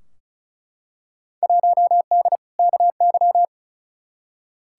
Morse code, heard as a keyed tone, about 34 wpm, 700 Hz. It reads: 1D KY
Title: Field Day Exchanges - 35wpm